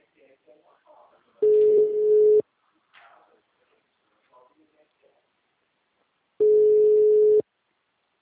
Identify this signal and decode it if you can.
Ringback tone (Europe, 425 Hz), with speech.